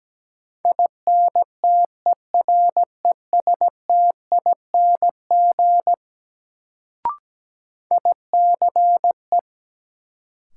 Morse code, heard as a keyed tone, about 17 wpm, 700 Hz. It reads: INTERESTING ICE
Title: Sentences from Top 1000 Words - New Word Drill - 17wpm